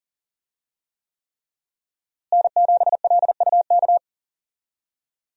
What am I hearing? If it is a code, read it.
Morse code, heard as a keyed tone, about 40 wpm, 700 Hz. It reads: N7LUK
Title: USA Call Signs - 5 Characters - Repeated 3x - 40wpm